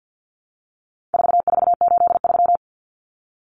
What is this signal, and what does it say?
Morse code, heard as a keyed tone, about 50 wpm, 700 Hz. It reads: 4483
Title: Common Cut Numbers - 4-Digit 50wpm